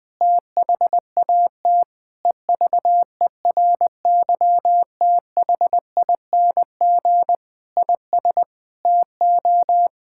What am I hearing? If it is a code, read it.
Morse code, heard as a keyed tone, about 21 wpm, 700 Hz. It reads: THAT EVERYTHING IS TO
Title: Sentences from Top 1000 Words - Review Part 6 - 20wpm